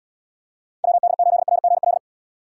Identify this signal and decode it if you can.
Morse code, heard as a keyed tone, about 78 wpm, 700 Hz. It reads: WD8RGL